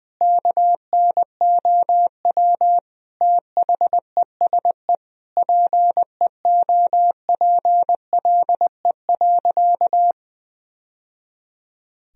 Morse code, heard as a keyed tone, about 21 wpm, 700 Hz. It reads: KNOW THESE PEOPLE.
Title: Sentences from Top 1000 Words - Review Part 1 - 20wpm